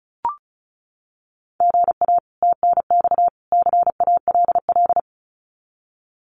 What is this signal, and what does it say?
Morse code, heard as a keyed tone, about 34 wpm, 700 Hz. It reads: GA TNX CALL
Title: Common QSO Phrases 35wpm